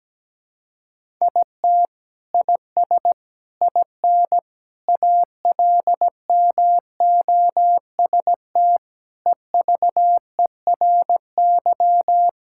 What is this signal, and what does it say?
Morse code, heard as a keyed tone, about 17 wpm, 700 Hz. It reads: IT IS IN ALMOST EVERY
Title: Sentences from Top 1000 Words - Review Part 5 - 17wpm